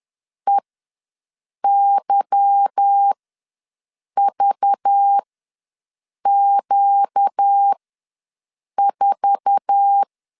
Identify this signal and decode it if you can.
Morse code, heard as a keyed tone, about 10 wpm, spaced as for 6 wpm, 800 Hz. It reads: EYVQ4
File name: LEZIONI / 10 SIGNS / 10C DOT ws 10wpm 4213